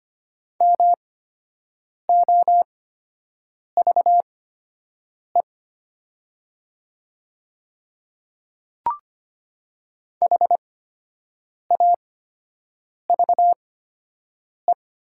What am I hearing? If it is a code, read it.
Morse code, heard as a keyed tone, about 25 wpm, spaced as for 7 wpm, 700 Hz. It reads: MOVE HAVE
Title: 8x Character Spacing - Top 200 Words - 25wpm